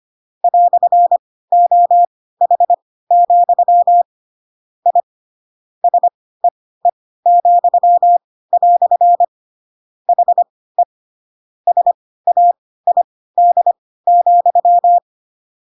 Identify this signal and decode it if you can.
Morse code, heard as a keyed tone, about 25 wpm, spaced as for 16 wpm, 700 Hz. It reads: "OH, I SEE," HE SAID,
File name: Pinocchio / Pinocchio ch1 10 25 ik1hge